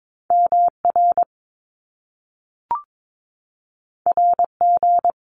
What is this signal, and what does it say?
Morse code, heard as a keyed tone, about 22 wpm, 700 Hz. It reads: MR RG
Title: Top Two Letter Combinations in Words 22wpm